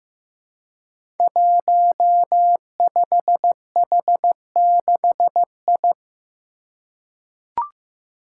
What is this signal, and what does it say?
Morse code, heard as a keyed tone, about 15 wpm, 700 Hz. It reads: 15H6I
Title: Five Challenging Letter-Number Combinations 15wpm